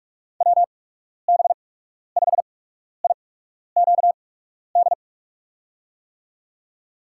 Morse code, heard as a keyed tone, about 46 wpm, spaced as for 12 wpm, 700 Hz. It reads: WB5IQD
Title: USA Call Signs - 6 Characters - 8x Character Spacing - 45wpm